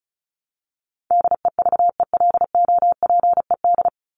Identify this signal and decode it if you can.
Morse code, heard as a keyed tone, about 34 wpm, 700 Hz. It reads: DEVELOPED
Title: ED Suffix - Top 100 Words with Base Words 35wpm